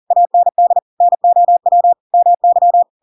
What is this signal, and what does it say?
Morse code, heard as a keyed tone, about 40 wpm, 700 Hz. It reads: AND NOW MY
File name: Pinocchio / Pinocchio ch7 40 40 ik1hge